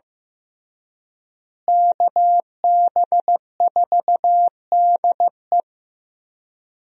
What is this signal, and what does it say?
Morse code, heard as a keyed tone, about 15 wpm, 700 Hz. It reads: KB4DE
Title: USA Call Signs - 5 Characters - Repeated 3x - 15wpm